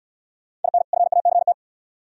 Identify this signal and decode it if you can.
Morse code, heard as a keyed tone, about 64 wpm, 700 Hz. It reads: IN FACT